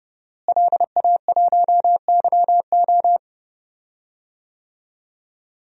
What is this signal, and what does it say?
Morse code, heard as a keyed tone, about 30 wpm, 700 Hz. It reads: LA1YO
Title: DXCC-100 Call Signs - 5 Characters - Repeated 3x - 30wpm